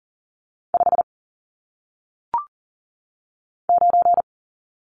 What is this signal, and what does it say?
Morse code, heard as a keyed tone, about 39 wpm, 700 Hz. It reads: 5 9
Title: Numbers - 1-Digit 40wpm